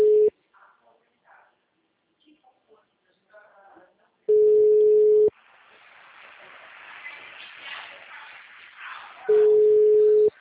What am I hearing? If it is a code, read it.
Ringback tone (Europe, 425 Hz), with speech.
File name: out- -2003- - - .61385